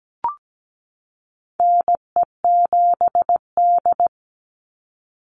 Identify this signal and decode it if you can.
Morse code, heard as a keyed tone, about 17 wpm, 700 Hz. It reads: NE7D
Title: Call Signs - 17wpm